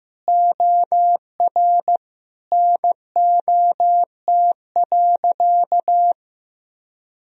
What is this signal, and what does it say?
Morse code, heard as a keyed tone, about 15 wpm, 700 Hz. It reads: OR NOT.
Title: Sentences from Top 1000 Words - Review Part 3 - 15wpm